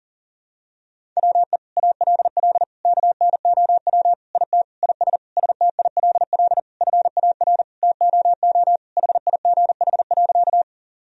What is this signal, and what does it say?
Morse code, heard as a keyed tone, about 40 wpm, 700 Hz. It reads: WE ALL KNOW IT IS STILL FAR TOO HIGH.
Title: Sentences from Top 1000 Words - Review Part 1 - 40wpm